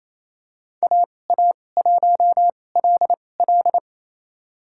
Morse code, heard as a keyed tone, about 28 wpm, spaced as for 20 wpm, 700 Hz. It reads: AA1LL
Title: USA Call Signs - 5 Characters - 2x Character Spacing - 28wpm